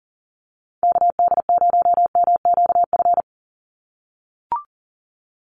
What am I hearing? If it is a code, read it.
Morse code, heard as a keyed tone, about 40 wpm, 700 Hz. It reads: KD0MQF
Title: USA Call Signs 40wpm